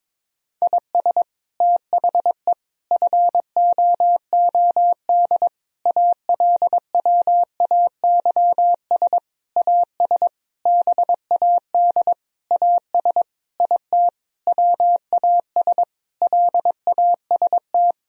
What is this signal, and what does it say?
Morse code, heard as a keyed tone, about 21 wpm, 700 Hz. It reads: IS THE FOOD ALWAYS AS BAD AS IT WAS LAST
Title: Sentences from Top 1000 Words - Review Part 1 - 22wpm